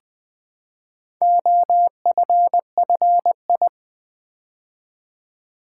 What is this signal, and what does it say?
Morse code, heard as a keyed tone, about 20 wpm, 700 Hz. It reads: OFFI
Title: Build A Word - 2 Letters At A Time - 8-Letter Words - 20wpm